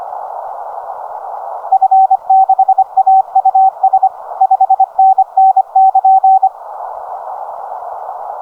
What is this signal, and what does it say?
Morse code, heard as a keyed tone, about 25 wpm, 770 Hz. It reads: F6AUS 5NN<KN>
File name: 1917 Doppler JW7QIA